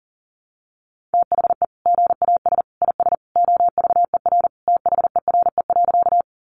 Morse code, heard as a keyed tone, about 37 wpm, 700 Hz. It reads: THE GAS IS OVER THERE.
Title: Sentences from Top 1100 Words 40wpm